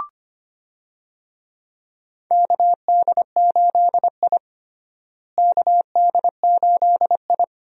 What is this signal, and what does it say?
Morse code, heard as a keyed tone, about 25 wpm, 700 Hz. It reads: KD8I KD8I
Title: USA Call Signs - 4 Characters - Repeated 2x - 25wpm